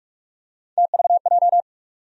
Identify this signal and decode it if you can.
Morse code, heard as a keyed tone, about 45 wpm, 700 Hz. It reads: TVJ
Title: USA Call Sign Postfixes - 3 Letters 45wpm